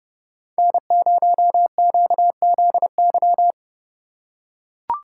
Morse code, heard as a keyed tone, about 30 wpm, 700 Hz. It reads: N0QZY
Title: USA Call Signs - 5 Characters - 4x Character Spacing - 30wpm